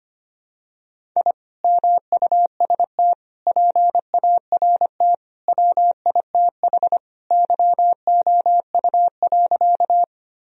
Morse code, heard as a keyed tone, about 25 wpm, 700 Hz. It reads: I MUST PART WITH YOU.
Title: Sentences from Top 1000 Words - Review Part 1 - 25wpm